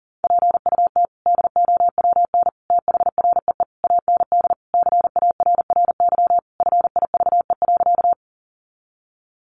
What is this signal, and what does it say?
Morse code, heard as a keyed tone, about 40 wpm, 700 Hz. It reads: PUT DOWN THREE AND CARRY FIVE.
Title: Sentences from Top 400 Words 40wpm